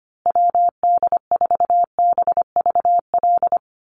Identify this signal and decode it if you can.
Morse code, heard as a keyed tone, about 25 wpm, 700 Hz. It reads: WD4BVL
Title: USA Call Signs Repeated 2x 25wpm